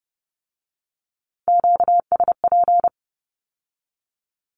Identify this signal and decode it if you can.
Morse code, heard as a keyed tone, about 30 wpm, 700 Hz. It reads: QSP
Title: Q-Codes with Spoken Meaning - 30wpm